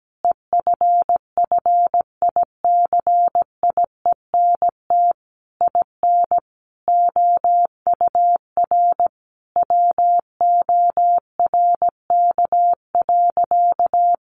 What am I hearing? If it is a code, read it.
Morse code, heard as a keyed tone, about 17 wpm, 700 Hz. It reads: EFFICIENT IN OUR WORK.
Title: Sentences from Top 1300 Words 17wpm